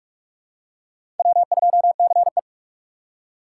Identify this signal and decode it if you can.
Morse code, heard as a keyed tone, about 41 wpm, 700 Hz. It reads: W2CE
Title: USA Call Signs - 4 Characters - 4x Character Spacing - 45wpm